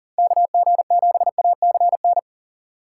Morse code, heard as a keyed tone, about 40 wpm, 700 Hz. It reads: KG7ACN
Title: USA Call Signs - 6 Characters - 3x Character Spacing - 40wpm